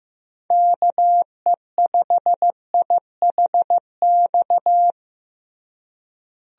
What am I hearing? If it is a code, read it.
Morse code, heard as a keyed tone, about 15 wpm, 700 Hz. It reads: KE5IHX
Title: USA Call Signs - 6 Characters - Repeated 3x - 15wpm